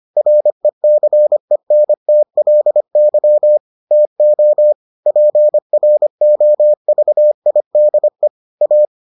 Morse code, heard as a keyed tone, about 25 wpm, 600 Hz. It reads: RECENTLY TO PROVIDE A